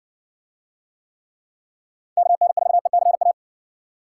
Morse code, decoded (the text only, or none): KN4ECA